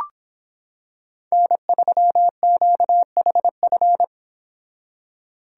Morse code, heard as a keyed tone, about 25 wpm, 700 Hz. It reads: N3QHF